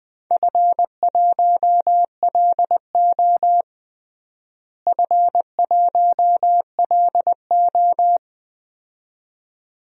Morse code, heard as a keyed tone, about 20 wpm, 700 Hz. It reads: F1LO F1LO
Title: DXCC-100 Call Signs - 4 Characters - Repeated 2x - 20wpm